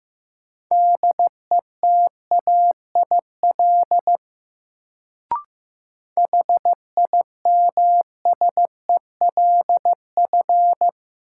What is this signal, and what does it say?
Morse code, heard as a keyed tone, about 16 wpm, 700 Hz. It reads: DETAIL HIMSELF
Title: Sentences from Top 1500 Words - New Word Drill - 15wpm